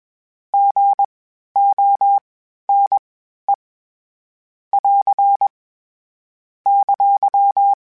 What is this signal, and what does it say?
Morse code, heard as a keyed tone, about 21 wpm, spaced as for 12 wpm, 800 Hz. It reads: GONE <AR> !
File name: Short Runs / Short Runs-mp3s / Gotten @20